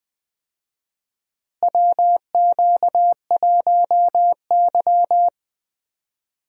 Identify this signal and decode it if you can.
Morse code, heard as a keyed tone, about 20 wpm, 700 Hz. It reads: WQ1Y
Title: USA Call Signs - 4 Characters - Repeated 2x - 20wpm